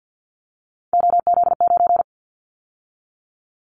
Morse code, heard as a keyed tone, about 50 wpm, 700 Hz. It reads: OZ9
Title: Callsign Prefixes for Top 100 DXCC - 50wpm